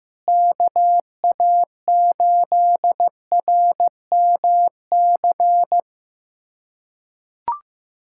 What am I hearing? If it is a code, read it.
Morse code, heard as a keyed tone, about 16 wpm, 700 Hz. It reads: KA8RMC
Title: USA Callsign DE Callsign - 6 Characters - 15wpm